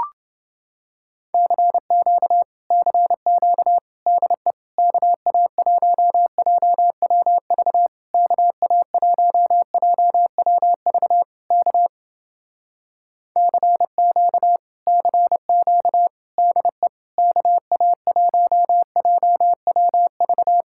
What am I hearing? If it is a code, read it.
Morse code, heard as a keyed tone, about 28 wpm, 700 Hz. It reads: CQ CQ DE KA1JWV KA1JWV K CQ CQ DE KA1JWV